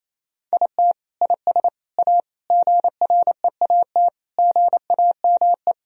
Morse code, heard as a keyed tone, about 28 wpm, 700 Hz. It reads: IT IS A GREAT GAME
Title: Sentences from Top 1000 Words - Review Part 3 - 28wpm